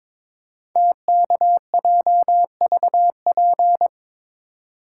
Morse code, heard as a keyed tone, about 22 wpm, 700 Hz. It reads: TKJVP
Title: Code Groups - 5 Letters - 22wpm